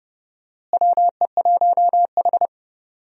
Morse code, heard as a keyed tone, about 32 wpm, 700 Hz. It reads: WE1H
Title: USA Call Signs - 4 Characters - Repeated 3x - 30wpm